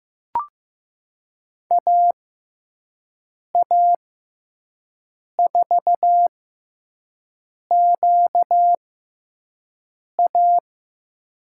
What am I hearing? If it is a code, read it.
Morse code, heard as a keyed tone, about 15 wpm, spaced as for 5 wpm, 700 Hz. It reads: AA4QA
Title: USA Call Signs - 5 Characters - 6x Character Spacing - 15wpm